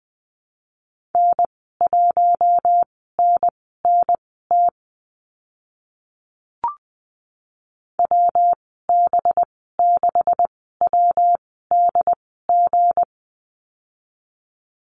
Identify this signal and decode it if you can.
Morse code, heard as a keyed tone, about 19 wpm, spaced as for 14 wpm, 700 Hz. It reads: N1NNT WB6WDG
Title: 186 Lesson 30g - Letter b - Call Signs 20wpm